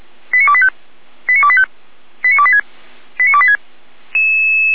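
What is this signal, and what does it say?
Five-tone selective call (ZVEI), heard as quick runs of five tones, tones of about 70 ms. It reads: 89285 89285 89286 89286
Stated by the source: ZVEI five-tone call 89262 received by SDR, logged by Feuerwehr Orlamünde.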